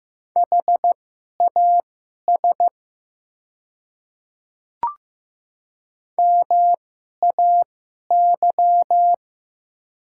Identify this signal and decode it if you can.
Morse code, heard as a keyed tone, about 15 wpm, spaced as for 11 wpm, 700 Hz. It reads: HAS MAY
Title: 2x Character Spacing - Top 100 Words - 15wpm